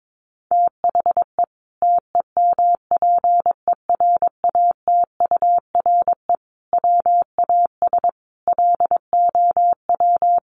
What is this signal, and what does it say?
Morse code, heard as a keyed tone, about 22 wpm, 700 Hz. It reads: THE TEMPERATURE WAS LOW
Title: Sentences from Top 1500 Words 22wpm